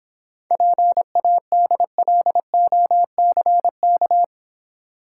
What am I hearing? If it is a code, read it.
Morse code, heard as a keyed tone, about 26 wpm, 700 Hz. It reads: PADLOCK